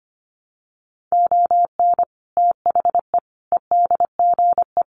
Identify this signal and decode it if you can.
Morse code, heard as a keyed tone, about 25 wpm, 700 Hz. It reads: ON THE EDGE
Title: Sets of 5 Words Encore2 25wpm